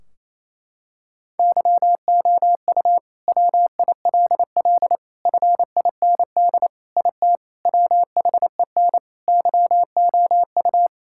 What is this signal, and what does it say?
Morse code, heard as a keyed tone, about 28 wpm, 700 Hz. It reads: YOU WILL FIND IT WHEN YOU
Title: Sentences from Top 1600 Words 28wpm